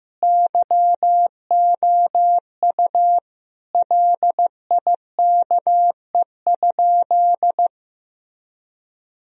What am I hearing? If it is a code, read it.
Morse code, heard as a keyed tone, about 15 wpm, 700 Hz. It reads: YOU LIKE?
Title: Sentences from Top 1000 Words - Review Part 3 - 15wpm